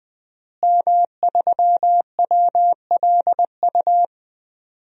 Morse code, heard as a keyed tone, about 20 wpm, 700 Hz. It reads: M3WLU
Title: DXCC-100 Call Signs - 5 Characters - Repeated 3x - 20wpm